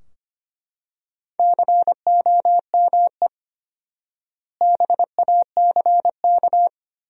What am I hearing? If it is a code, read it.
Morse code, heard as a keyed tone, about 25 wpm, 700 Hz. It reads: COME BACK
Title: 4x Word-Spacing - Sentences from Top 500 Words - 25wpm